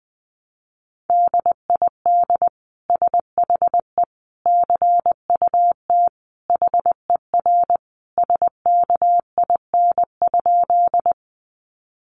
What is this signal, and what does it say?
Morse code, heard as a keyed tone, about 19 wpm, 700 Hz. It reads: DID SHE CUT HER SKIN?
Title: Sentences from Top 1300 Words 20wpm